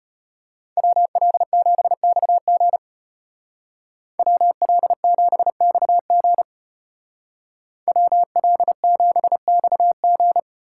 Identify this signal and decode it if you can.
Morse code, heard as a keyed tone, about 34 wpm, 700 Hz. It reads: WL7XG WL7XG WL7XG